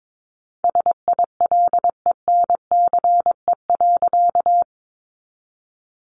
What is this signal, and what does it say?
Morse code, heard as a keyed tone, about 22 wpm, 700 Hz. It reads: SILENCE.
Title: A Sound Of Thunder 22wpm